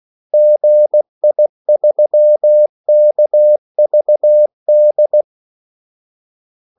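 Morse code, heard as a keyed tone, about 16 wpm, 600 Hz. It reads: GI3KVD
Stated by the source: Morse call sign practice at 16 wpm.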